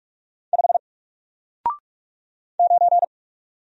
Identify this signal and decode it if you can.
Morse code, heard as a keyed tone, about 45 wpm, 700 Hz. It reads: 5 9
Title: Single Letter-Number Alt 45wpm